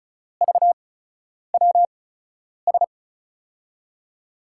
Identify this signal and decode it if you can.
Morse code, heard as a keyed tone, about 36 wpm, spaced as for 10 wpm, 700 Hz. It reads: VWS
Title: USA Call Signs - 5 Characters - 8x Character Spacing - 35wpm